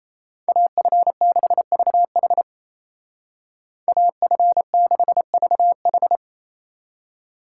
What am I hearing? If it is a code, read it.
Morse code, heard as a keyed tone, about 34 wpm, 700 Hz. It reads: AF6VH AF6VH
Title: Speed Racing - USA 2x2 Call Signs 28wpm